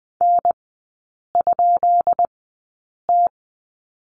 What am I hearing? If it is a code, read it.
Morse code, heard as a keyed tone, about 20 wpm, spaced as for 8 wpm, 700 Hz. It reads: N?T
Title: 159 Lesson 26e - Letter g - Three Characters 20wpm